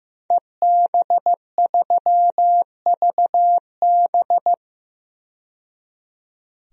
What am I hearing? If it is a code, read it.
Morse code, heard as a keyed tone, about 15 wpm, 700 Hz. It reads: EB3VB
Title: DXCC-100 Call Signs - 5 Characters - 15wpm